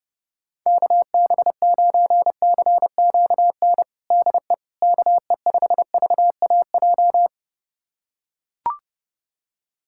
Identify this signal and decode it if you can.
Morse code, heard as a keyed tone, about 30 wpm, 700 Hz. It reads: KB9CQN DE KE5VAJ
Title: USA Callsign DE Callsign - 6 Characters - 30wpm